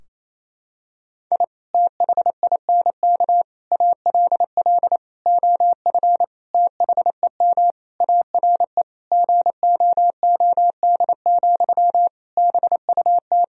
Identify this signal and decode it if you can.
Morse code, heard as a keyed tone, about 28 wpm, 700 Hz. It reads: I THINK ALL OF THEM ARE GOOD, BUT
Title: Sentences from Top 1000 Words - Review Part 4 - 28wpm